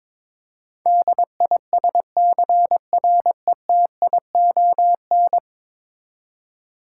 Morse code, heard as a keyed tone, about 22 wpm, 700 Hz. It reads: DISCRETION
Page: morse-code-ninja-practice-22wpm